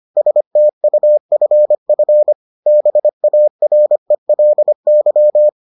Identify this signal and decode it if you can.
Morse code, heard as a keyed tone, about 25 wpm, 600 Hz. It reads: STUFF BARELY